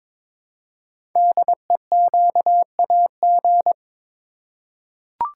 Morse code, heard as a keyed tone, about 22 wpm, 700 Hz. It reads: DEQAG
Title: Code Groups - 5 Letters - 22wpm